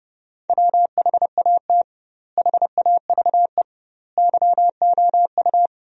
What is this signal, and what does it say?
Morse code, heard as a keyed tone, about 32 wpm, 700 Hz. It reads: WHAT HAVE YOU
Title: 2x Word-Spacing - Sentences from Top 500 Words - 30wpm